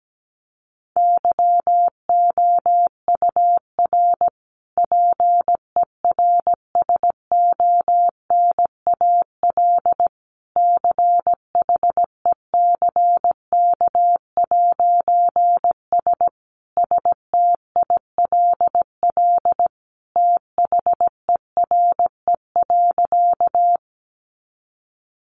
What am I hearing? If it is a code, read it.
Morse code, heard as a keyed tone, about 17 wpm, 700 Hz. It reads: YOUR PERSONAL CHECK'S STILL THERE.
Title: A Sound Of Thunder 17wpm